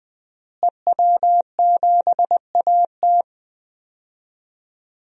Morse code, heard as a keyed tone, about 20 wpm, 700 Hz. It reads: EW7AT
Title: DXCC-100 Call Signs - 5 Characters - Repeated 3x - 20wpm